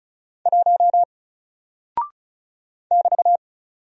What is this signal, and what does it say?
Morse code, heard as a keyed tone, about 36 wpm, 700 Hz. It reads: 1 <BT>
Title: Single Character (Letter:Number:Punctuation:Prosign) Alt 35wpm